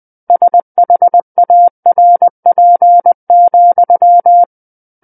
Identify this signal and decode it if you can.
Morse code, heard as a keyed tone, about 20 wpm, 700 Hz. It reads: SHARP,
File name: 20 WPM